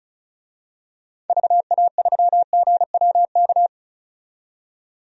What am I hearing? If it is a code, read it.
Morse code, heard as a keyed tone, about 36 wpm, 700 Hz. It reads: VA3GWK
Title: DXCC-100 Call Signs - 6 Characters - 35wpm